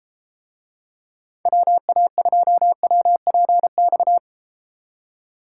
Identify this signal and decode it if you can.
Morse code, heard as a keyed tone, about 33 wpm, 700 Hz. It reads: WA2WPX